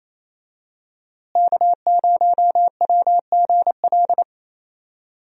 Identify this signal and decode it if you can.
Morse code, heard as a keyed tone, about 27 wpm, 700 Hz. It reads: K0WGL
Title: USA Call Signs - 5 Characters - Repeated 3x - 28wpm